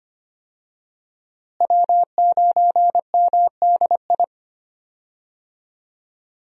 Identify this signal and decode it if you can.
Morse code, heard as a keyed tone, about 25 wpm, 700 Hz. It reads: W9MDI
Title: USA Call Signs - 5 Characters - Repeated 3x - 25wpm